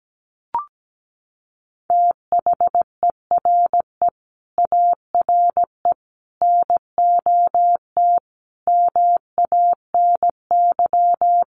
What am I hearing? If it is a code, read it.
Morse code, heard as a keyed tone, about 17 wpm, 700 Hz. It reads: THERE ARE NOT MANY
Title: Sentences from Top 1300 Words 17wpm